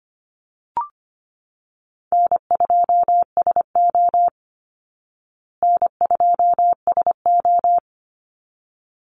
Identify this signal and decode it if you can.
Morse code, heard as a keyed tone, about 25 wpm, 700 Hz. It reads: N2SO N2SO
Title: Easy Call Signs Repeated 2x - 25wpm